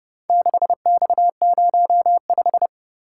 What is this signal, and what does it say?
Morse code, heard as a keyed tone, about 25 wpm, 700 Hz. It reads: 6X05